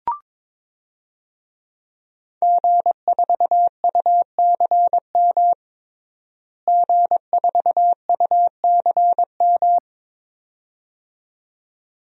Morse code, heard as a keyed tone, about 22 wpm, 700 Hz. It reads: G4UCM G4UCM
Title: DXCC-100 Call Signs - 5 Characters - Repeated 2x - 22wpm